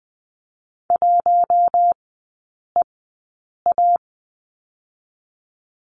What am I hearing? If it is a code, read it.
Morse code, heard as a keyed tone, about 19 wpm, 700 Hz. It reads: 1 E A
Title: 047 Lesson 9e - Number 4 - Three Characters 20wpm